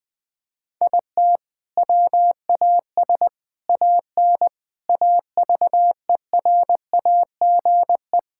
Morse code, heard as a keyed tone, about 20 wpm, 700 Hz. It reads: IT WAS AN AVERAGE
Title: Sentences from Top 1000 Words - Review Part 6 - 20wpm